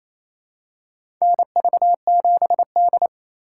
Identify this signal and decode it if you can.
Morse code, heard as a keyed tone, about 28 wpm, 700 Hz. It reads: NV7D
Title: USA Call Signs - 4 Characters - 3x Character Spacing - 28wpm